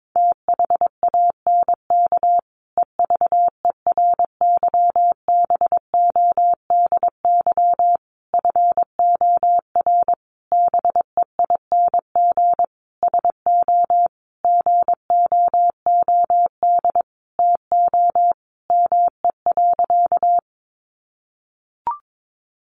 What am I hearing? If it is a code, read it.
Morse code, heard as a keyed tone, about 22 wpm, 700 Hz. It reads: THANK EVERYBODY FOR BEING SO GOOD TO ME.
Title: Sentences from Top 1400 Words 22wpm